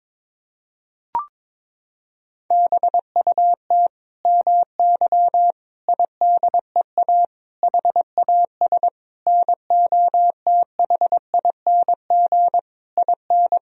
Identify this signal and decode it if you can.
Morse code, heard as a keyed tone, about 22 wpm, 700 Hz. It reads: BUT MY IDEA HAS NOTHING IN
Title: Sentences from Top 1000 Words - Review Part 3 - 22wpm